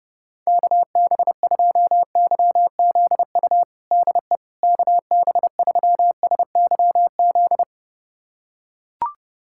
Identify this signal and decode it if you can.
Morse code, heard as a keyed tone, about 30 wpm, 700 Hz. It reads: KB2YZU DE KB3SYZ
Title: USA Callsign DE Callsign 30wpm